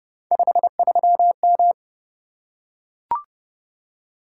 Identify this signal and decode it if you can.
Morse code, heard as a keyed tone, about 28 wpm, 700 Hz. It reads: 53M